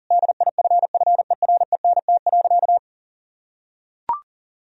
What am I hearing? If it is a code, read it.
Morse code, heard as a keyed tone, about 40 wpm, 700 Hz. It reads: DIFFERENT.
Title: Sentences from Top 1000 Words - Review Part 3 - 40wpm